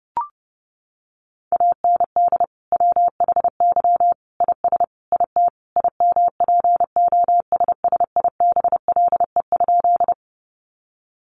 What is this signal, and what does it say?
Morse code, heard as a keyed tone, about 30 wpm, 700 Hz. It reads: AND WHY IS IT IMPOSSIBLE?
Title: Sentences from Top 1100 Words 30wpm